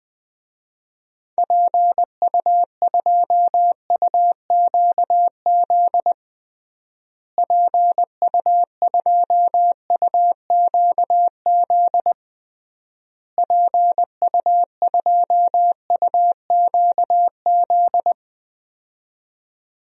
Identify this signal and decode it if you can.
Morse code, heard as a keyed tone, about 20 wpm, 700 Hz. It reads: PU2UQZ PU2UQZ PU2UQZ
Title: DXCC-100 Call Signs - 6 Characters - Repeated 3x - 20wpm